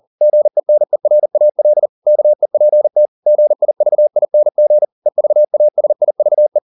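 Morse code, heard as a keyed tone, about 40 wpm, 600 Hz. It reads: GENERAL KEPT GIVING EVASIVE
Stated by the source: Morse code at 40 wpm, practice text, CW med Gnister.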